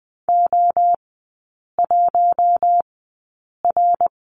Morse code, heard as a keyed tone, about 20 wpm, spaced as for 8 wpm, 700 Hz. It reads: O1R
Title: 053 Lesson 10e - Letter r - Three Characters 20wpm